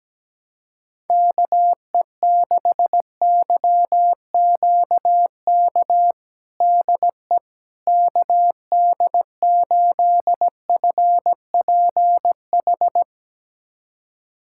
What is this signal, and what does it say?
Morse code, heard as a keyed tone, about 17 wpm, 700 Hz. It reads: KE6YQK DE KD8FPH
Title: USA Callsign DE Callsign 17wpm